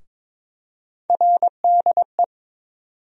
Morse code, heard as a keyed tone, about 22 wpm, 700 Hz. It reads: RDE
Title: USA Call Sign Postfixes - 3 Letters 22wpm